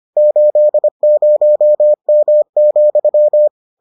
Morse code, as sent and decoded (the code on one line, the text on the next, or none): ---.. ----- -- --..--
80M,